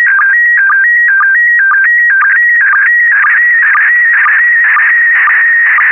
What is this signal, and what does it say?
SSTV (slow-scan television), PD 120.
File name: ISS-SSTV / SSTV-Audio-ISS-Bild-09